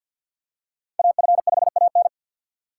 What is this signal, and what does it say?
Morse code, heard as a keyed tone, about 48 wpm, 700 Hz. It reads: AF5AN
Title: USA Call Signs - 5 Characters - 2x Character Spacing - 50wpm